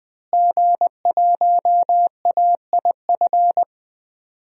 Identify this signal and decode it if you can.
Morse code, heard as a keyed tone, about 20 wpm, 700 Hz. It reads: G1AIF
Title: DXCC-100 Call Signs - 5 Characters - Repeated 3x - 20wpm